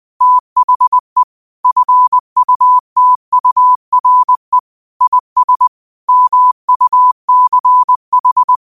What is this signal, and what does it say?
Morse code, heard as a keyed tone, about 20 wpm, 1000 Hz. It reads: THE FUTURE IS MUCH